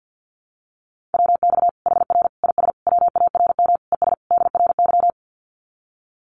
Morse code, heard as a keyed tone, about 50 wpm, 700 Hz. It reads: WX HR IS WARM ES DRY
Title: Common QSO Phrases 50wpm